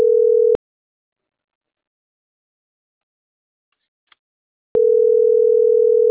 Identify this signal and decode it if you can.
Ringback tone (North America), 440 + 480 Hz.
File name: CENTRAL SOTOVA / out roberto.miranda